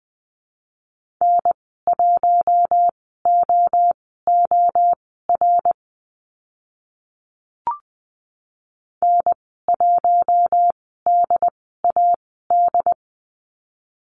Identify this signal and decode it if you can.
Morse code, heard as a keyed tone, about 20 wpm, spaced as for 14 wpm, 700 Hz. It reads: N1OOR N1DAD
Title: 069 Lesson 12g - Letter d - Call Signs 20wpm